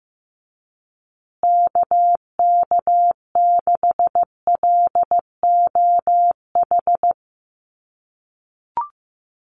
Morse code, heard as a keyed tone, about 15 wpm, 700 Hz. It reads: KK6LOH
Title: USA Call Signs Repeated 2x 15wpm